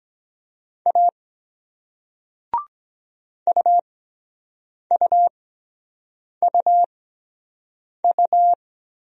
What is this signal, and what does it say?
Morse code, heard as a keyed tone, about 23 wpm, spaced as for 7 wpm, 700 Hz. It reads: A UUUU